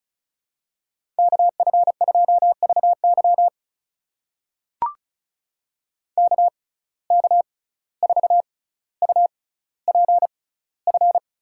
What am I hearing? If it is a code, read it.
Morse code, heard as a keyed tone, about 36 wpm, spaced as for 17 wpm, 700 Hz. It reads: KF2VY KK4UPF
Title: USA Call Signs - 6x Character Spacing - 35wpm